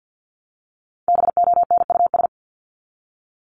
Morse code, heard as a keyed tone, about 50 wpm, 700 Hz. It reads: BONUS
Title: Top 1751-2000 Words 50wpm